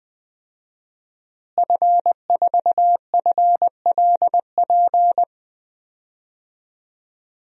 Morse code, heard as a keyed tone, about 20 wpm, 700 Hz. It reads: F4FLP
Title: DXCC-100 Call Signs - 5 Characters - Repeated 3x - 20wpm